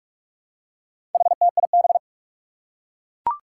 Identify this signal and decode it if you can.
Morse code, heard as a keyed tone, about 44 wpm, 700 Hz. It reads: HTIB
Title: Four Challenging Letter-Number Combinations 45wpm